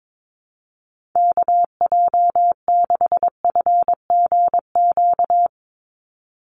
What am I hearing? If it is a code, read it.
Morse code, heard as a keyed tone, about 22 wpm, 700 Hz. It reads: KJ6FGQ
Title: USA Call Signs 22wpm